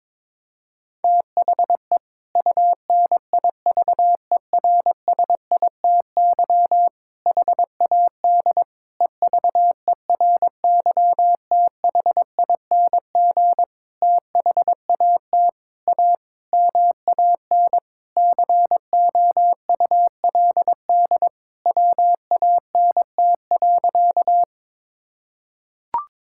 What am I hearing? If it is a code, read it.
Morse code, heard as a keyed tone, about 22 wpm, 700 Hz. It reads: THE UNIVERSITY HAD EVERYTHING THAT A MAN COULD WANT.
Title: Sentences from Top 1000 Words - Review Part 5 - 22wpm